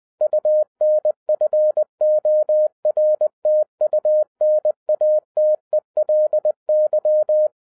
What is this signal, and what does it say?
Morse code, heard as a keyed tone, about 20 wpm, 610 Hz. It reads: UNFORTUNATELY
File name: 2011-10-mp3 / n03-TARO JR0QWW 20wpm